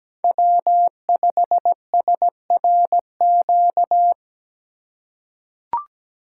Morse code, heard as a keyed tone, about 17 wpm, 700 Hz. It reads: W5SRQ
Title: USA Call Signs - 5 Characters - 8x Character Spacing - 17wpm